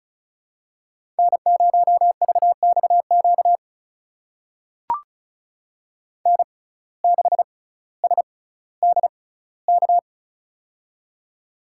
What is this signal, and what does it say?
Morse code, heard as a keyed tone, about 36 wpm, spaced as for 18 wpm, 700 Hz. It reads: N0VXQ N6SDK
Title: USA Call Signs - 6x Character Spacing - 35wpm